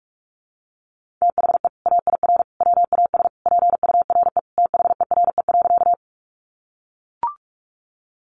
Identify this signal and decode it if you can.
Morse code, heard as a keyed tone, about 45 wpm, 700 Hz. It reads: THE AIR WAS PURE THERE.
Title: Sentences from Top 1500 Words 45wpm